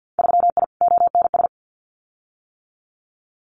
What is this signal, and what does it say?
Morse code, heard as a keyed tone, about 50 wpm, 700 Hz. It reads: 3I ONS
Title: Winter Field Day Exchanges Encore 50wpm